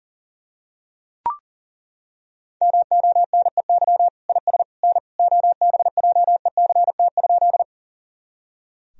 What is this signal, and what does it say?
Morse code, heard as a keyed tone, about 40 wpm, 700 Hz. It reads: MONEY IS N OBJECT?
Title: Sentences from Top 500 Words and Abbreviations 40wpm